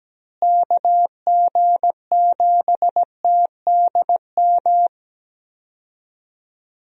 Morse code, heard as a keyed tone, about 17 wpm, 700 Hz. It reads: KG7TDM
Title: USA Call Signs - 6 Characters - Repeated 2x - 17wpm